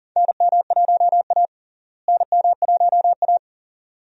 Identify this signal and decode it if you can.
Morse code, heard as a keyed tone, about 40 wpm, 700 Hz. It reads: NM1A NM1A
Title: USA Call Signs - 4 Characters - Repeated 2x - 40wpm